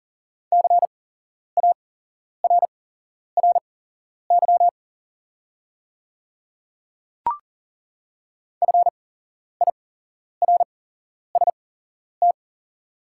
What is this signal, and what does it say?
Morse code, heard as a keyed tone, about 40 wpm, spaced as for 11 wpm, 700 Hz. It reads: CARRY FIRST
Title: 8x Character Spacing - Top 300 Words - 40wpm